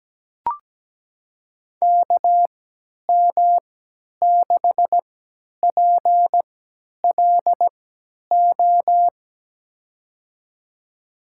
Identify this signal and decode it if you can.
Morse code, heard as a keyed tone, about 17 wpm, spaced as for 10 wpm, 700 Hz. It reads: KM6PLO
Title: USA Call Signs - 3x Character Spacing - 17wpm